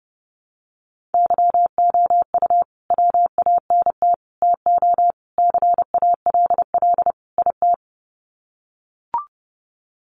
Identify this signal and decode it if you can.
Morse code, heard as a keyed tone, about 28 wpm, 700 Hz. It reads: YOU WANT TO CALL IT
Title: Sets of 5 Words Encore2 30wpm